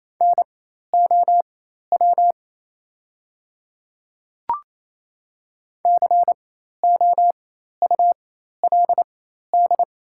Morse code, heard as a keyed tone, about 28 wpm, spaced as for 13 wpm, 700 Hz. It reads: NOW COULD
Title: 4x Character Spacing - Top 100 Words - 28wpm